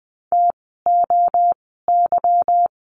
Morse code, heard as a keyed tone, about 20 wpm, spaced as for 14 wpm, 700 Hz. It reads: TOY
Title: 146 Lesson 24f - Letter y - Words 20wpm